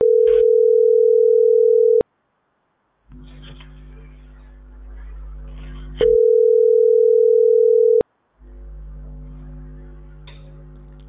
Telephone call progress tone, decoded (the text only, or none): ringback tone